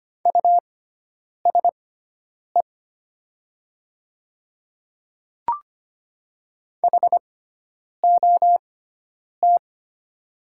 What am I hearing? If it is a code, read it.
Morse code, heard as a keyed tone, about 25 wpm, spaced as for 9 wpm, 700 Hz. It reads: USE HOT
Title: 6x Character Spacing - Top 100 Words - 25wpm